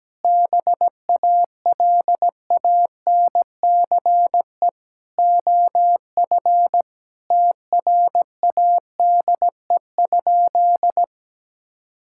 Morse code, heard as a keyed tone, about 17 wpm, 700 Hz. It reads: BALANCE OF TRADE?
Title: Sentences from Top 1000 Words - Review Part 6 - 17wpm